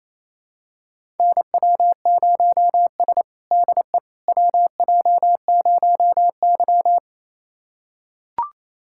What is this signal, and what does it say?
Morse code, heard as a keyed tone, about 28 wpm, 690 Hz. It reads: NW0S DE WJ0Y
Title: USA Callsign DE Callsign - 4 Characters - 28wpm